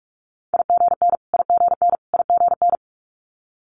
Morse code, heard as a keyed tone, about 46 wpm, 700 Hz. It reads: IGN IGN IGN
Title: Multi-letter Phonemes Repeated 3x 45wpm